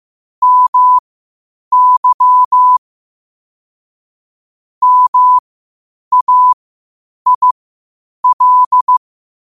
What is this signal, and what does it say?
Morse code, heard as a keyed tone, about 15 wpm, spaced as for 8 wpm, 1000 Hz. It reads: MY MAIL
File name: qotdfiles / 8wpm